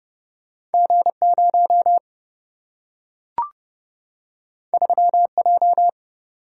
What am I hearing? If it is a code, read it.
Morse code, heard as a keyed tone, about 30 wpm, 700 Hz. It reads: G0 3J
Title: Single-Letter Single-Number 30wpm